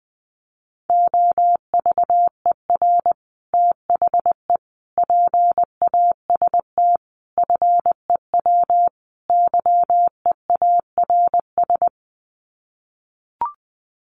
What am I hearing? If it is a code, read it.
Morse code, heard as a keyed tone, about 20 wpm, 700 Hz. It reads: OVER THE PAST FEW YEARS
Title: Sets of 5 Words Encore 20wpm